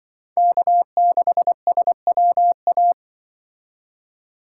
Morse code, heard as a keyed tone, about 24 wpm, 700 Hz. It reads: K6SWA
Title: Speed Racing - USA 1x3 Call Signs 20wpm